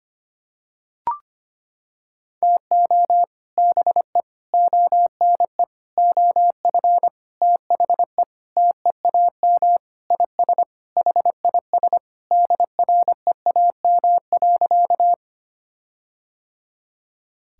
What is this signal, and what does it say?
Morse code, heard as a keyed tone, about 25 wpm, 700 Hz. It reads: TO BE ONE OF THE TEAM IS HIS DREAM.
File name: practice / 25wpm / Sentences with 4-Word Phrases - 25wpm